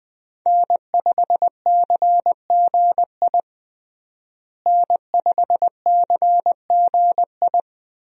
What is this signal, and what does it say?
Morse code, heard as a keyed tone, about 20 wpm, 700 Hz. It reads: N5CGI N5CGI
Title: USA Call Signs - 5 Characters - Repeated 2x - 20wpm